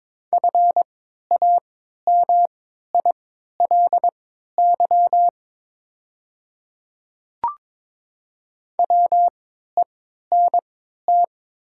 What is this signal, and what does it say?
Morse code, heard as a keyed tone, about 22 wpm, spaced as for 12 wpm, 700 Hz. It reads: FAMILY WENT
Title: 3x Character Spacing - Top 500 Words - 22wpm